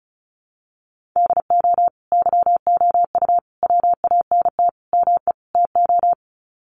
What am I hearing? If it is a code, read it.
Morse code, heard as a keyed tone, about 34 wpm, 700 Hz. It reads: DO YOU WANT ME TO
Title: Sets of 5 Words 35wpm